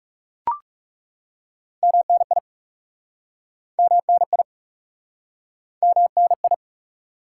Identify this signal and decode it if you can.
Morse code, heard as a keyed tone, about 40 wpm, 700 Hz. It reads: MNI MNI MNI